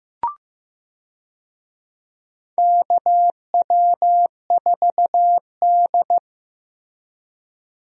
Morse code, heard as a keyed tone, about 15 wpm, 700 Hz. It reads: KW4D
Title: USA Call Signs - 4 Characters - Repeated 3x - 15wpm